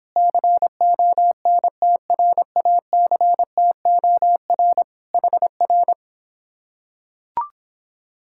Morse code, heard as a keyed tone, about 26 wpm, 700 Hz. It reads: CONTRACTOR HR